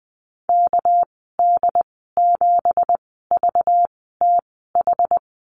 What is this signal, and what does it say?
Morse code, heard as a keyed tone, about 20 wpm, spaced as for 14 wpm, 700 Hz. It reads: KD7VTH
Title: 231 Lesson 37g - Letter x - Call Signs 20wpm